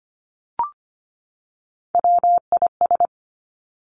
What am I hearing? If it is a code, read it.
Morse code, heard as a keyed tone, about 25 wpm, 700 Hz. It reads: WIS
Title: Three Challenging Letter-Number Combinations 25wpm